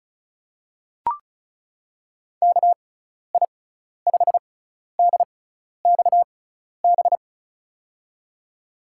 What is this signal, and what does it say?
Morse code, heard as a keyed tone, about 35 wpm, spaced as for 12 wpm, 700 Hz. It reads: KI5DXB
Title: USA Call Signs - 6 Characters - 6x Character Spacing - 35wpm